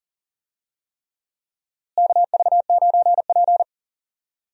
Morse code, heard as a keyed tone, about 40 wpm, 700 Hz. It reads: KV9P
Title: USA Call Signs - 4 Characters - Repeated 3x - 40wpm